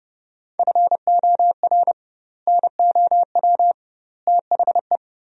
Morse code, heard as a keyed tone, about 30 wpm, 700 Hz. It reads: FOR NOW THE
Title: 2x Word-Spacing - Sentences from Top 200 Words - 30wpm